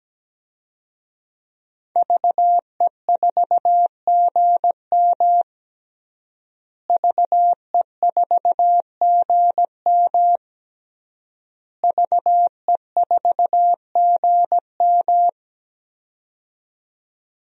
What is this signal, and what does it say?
Morse code, heard as a keyed tone, about 17 wpm, 700 Hz. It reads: VE4GM VE4GM VE4GM
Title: DXCC-100 Call Signs - 5 Characters - Repeated 3x - 17wpm